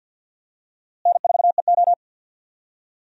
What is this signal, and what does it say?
Morse code, heard as a keyed tone, about 47 wpm, 700 Hz. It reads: N4EO
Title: USA Call Signs - 4 Characters - Repeated 2x - 50wpm